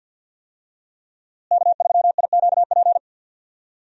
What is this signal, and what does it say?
Morse code, heard as a keyed tone, about 47 wpm, 700 Hz. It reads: K3IQP
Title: USA Call Signs - 5 Characters - Repeated 3x - 50wpm